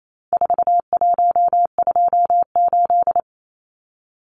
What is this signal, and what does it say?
Morse code, heard as a keyed tone, about 28 wpm, 700 Hz. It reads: 4128
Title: Numbers - 4-Digit 28wpm